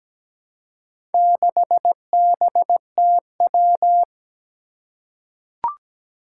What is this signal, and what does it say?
Morse code, heard as a keyed tone, about 17 wpm, 700 Hz. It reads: 6BTW
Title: Four Challenging Letter-Number Combinations 17wpm